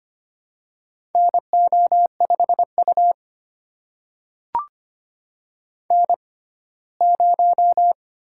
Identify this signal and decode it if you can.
Morse code, heard as a keyed tone, about 25 wpm, spaced as for 17 wpm, 700 Hz. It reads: NO5U N0
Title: USA Call Signs - 4 Characters - 6x Character Spacing - 25wpm